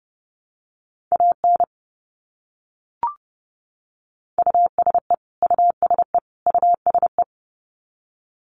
Morse code, heard as a keyed tone, about 29 wpm, 700 Hz. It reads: AN USE USE USE
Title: Top 200 Words Repeated 30wpm